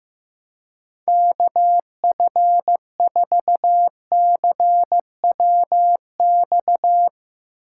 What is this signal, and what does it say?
Morse code, heard as a keyed tone, about 15 wpm, 700 Hz. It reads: KF4CWX
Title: Speed Racing - USA 2x3 Call Signs 15wpm